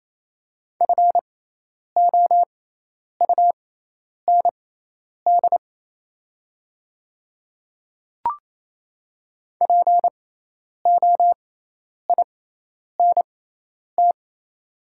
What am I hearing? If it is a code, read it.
Morse code, heard as a keyed tone, about 28 wpm, spaced as for 10 wpm, 700 Hz. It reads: FOUND POINT
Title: 6x Character Spacing - Top 300 Words - 28wpm